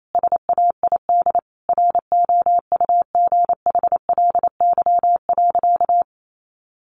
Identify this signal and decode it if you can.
Morse code, heard as a keyed tone, about 28 wpm, 700 Hz. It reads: SAID ROUGHLY.
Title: Sentences from Top 1500 Words 28wpm